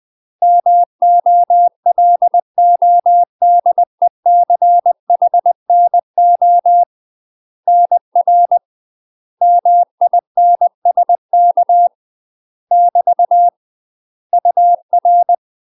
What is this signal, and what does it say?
Morse code, heard as a keyed tone, about 20 wpm, 700 Hz. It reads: MOLODECHNO NR MINSK <BT> UR
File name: UK MASTER QSOs TRAINING / 20WPM / UK MASTER QSOs 0061